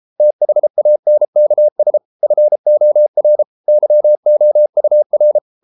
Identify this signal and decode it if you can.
Morse code, heard as a keyed tone, about 33 wpm, 600 Hz. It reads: THANKS FOR YOUR